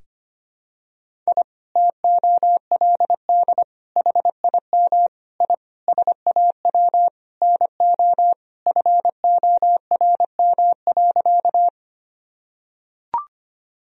Morse code, heard as a keyed tone, about 25 wpm, 700 Hz. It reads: I TOLD HIM I SAW NO FORM.
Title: Sentences from Top 1000 Words - Review Part 1 - 25wpm